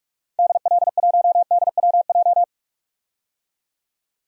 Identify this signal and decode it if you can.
Morse code, heard as a keyed tone, about 47 wpm, 690 Hz. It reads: DL1DWJ
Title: DXCC-100 Call Signs - 6 Characters - 45wpm